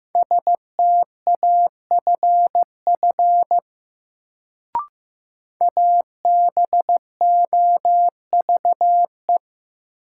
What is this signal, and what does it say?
Morse code, heard as a keyed tone, about 15 wpm, 700 Hz. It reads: STAFF ABOVE
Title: Sentences from Top 1200 Words - New Word Drill Alt - 15wpm